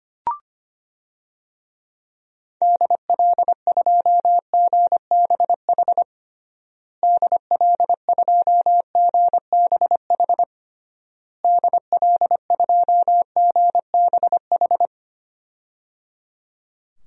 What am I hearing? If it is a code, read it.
Morse code, heard as a keyed tone, about 25 wpm, 700 Hz. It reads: DL2GBH DL2GBH DL2GBH
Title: DXCC-100 Call Signs - 6 Characters - Repeated 3x - 25wpm